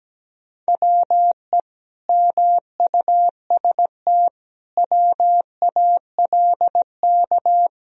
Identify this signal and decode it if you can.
Morse code, heard as a keyed tone, about 17 wpm, 700 Hz. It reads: WE MUST WALK
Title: Sentences from Top 1000 Words - Review Part 2 - 17wpm